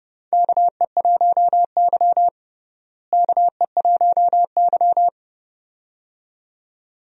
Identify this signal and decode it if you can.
Morse code, heard as a keyed tone, about 30 wpm, 700 Hz. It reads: KE1Y KE1Y
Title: USA Call Signs - 4 Characters - Repeated 2x - 30wpm